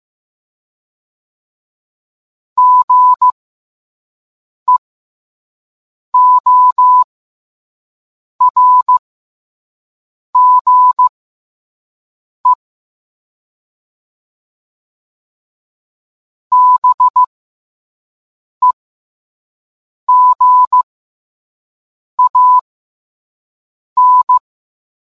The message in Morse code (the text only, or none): GEORGE BEGAN